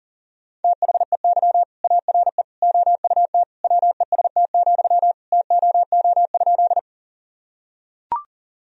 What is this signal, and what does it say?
Morse code, heard as a keyed tone, about 40 wpm, 700 Hz. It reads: THEY ARE OUT WEST, TOO?
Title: Sentences from Top 1000 Words - Review Part 2 - 40wpm